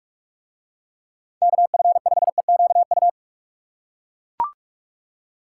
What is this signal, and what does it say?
Morse code, heard as a keyed tone, about 41 wpm, 700 Hz. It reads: KF5EQU